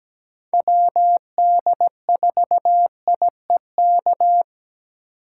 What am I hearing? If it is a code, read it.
Morse code, heard as a keyed tone, about 17 wpm, 700 Hz. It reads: WD4IEK
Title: USA Call Signs - 6 Characters - 4x Character Spacing - 17wpm